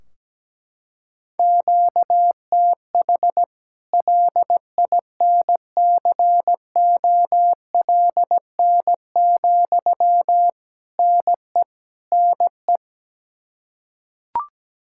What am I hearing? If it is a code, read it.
Morse code, heard as a keyed tone, about 17 wpm, 700 Hz. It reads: QTH LINCOLN, NE NE
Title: Practice QSO - Exchange 1 - Beginning - 17wpm